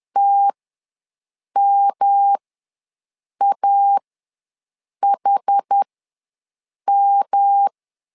Morse code, heard as a keyed tone, about 10 wpm, spaced as for 6 wpm, 800 Hz. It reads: TMAHM
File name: LEZIONI / 3 MWRO / 3 M ws 10wpm 2121